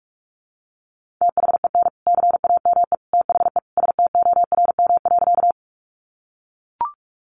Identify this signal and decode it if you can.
Morse code, heard as a keyed tone, about 44 wpm, 700 Hz. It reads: THEN CAME THE STORM.
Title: Sentences from Top 1500 Words 45wpm